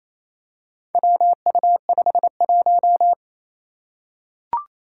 Morse code, heard as a keyed tone, about 28 wpm, 700 Hz. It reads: WU51
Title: Four Challenging Letter-Number Combinations 28wpm